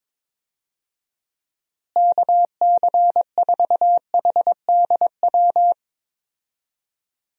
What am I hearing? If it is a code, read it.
Morse code, heard as a keyed tone, about 22 wpm, 700 Hz. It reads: KC4HDW
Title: USA Call Signs - 6 Characters - Repeated 2x - 22wpm